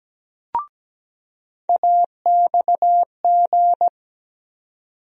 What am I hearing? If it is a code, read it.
Morse code, heard as a keyed tone, about 17 wpm, 700 Hz. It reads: AXG
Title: USA Call Sign Postfixes - 3 Letters 17wpm